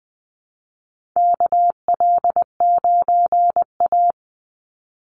Morse code, heard as a keyed tone, about 20 wpm, 700 Hz. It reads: KL9A
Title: Call Signs Repeated 3x - 20wpm